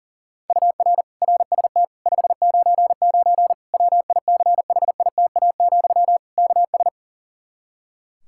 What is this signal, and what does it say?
Morse code, heard as a keyed tone, about 40 wpm, 700 Hz. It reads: UR RST 599 WICHITA, KS
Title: Practice QSO - Exchange 1 - Beginning - 40wpm